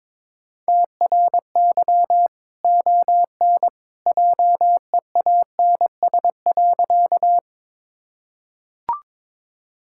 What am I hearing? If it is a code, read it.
Morse code, heard as a keyed tone, about 22 wpm, 700 Hz. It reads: TRY ON JEANS.
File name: practice / 22wpm / Easy Sentences with Hints - 22wpm